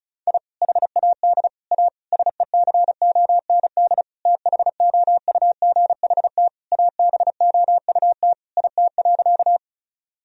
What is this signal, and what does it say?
Morse code, heard as a keyed tone, about 36 wpm, 700 Hz. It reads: I HAD A SECOND THOUGHT ABOUT IT.
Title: Sentences from Top 1000 Words - Review Part 1 - 35wpm